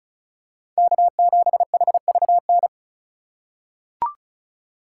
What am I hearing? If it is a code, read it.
Morse code, heard as a keyed tone, about 36 wpm, 700 Hz. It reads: K7HVN
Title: USA Call Signs - 5 Characters - 2x Character Spacing - 35wpm